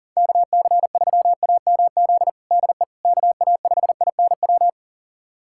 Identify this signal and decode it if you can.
Morse code, heard as a keyed tone, about 40 wpm, 700 Hz. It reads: KC3AMZ DE KA5INW
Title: USA Callsign DE Callsign - 6 Characters - 40wpm